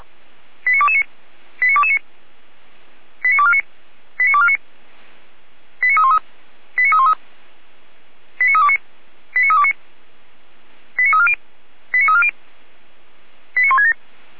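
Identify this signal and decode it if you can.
Five-tone selective call (ZVEI), heard as quick runs of five tones, tones of about 70 ms. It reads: 89229 89229 89249 89249 89313 89313 89329 89329 89340 89340 89167
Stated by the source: ZVEI five-tone selective call 89240 received by SDR, on the Feuerwehr Orlamünde server.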